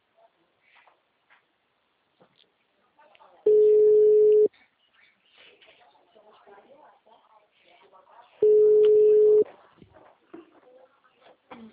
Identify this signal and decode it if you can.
Ringback tone (Europe, 425 Hz), with speech.